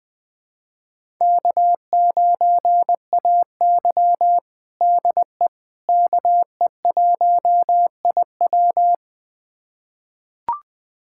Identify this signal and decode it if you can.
Morse code, heard as a keyed tone, about 20 wpm, 700 Hz. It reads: K9AY DE KE1IW
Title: USA Callsign DE Callsign 20wpm